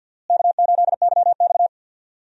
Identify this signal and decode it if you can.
Morse code, heard as a keyed tone, about 49 wpm, 700 Hz. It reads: K8YX
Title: USA Call Signs - 4 Characters - Repeated 2x - 50wpm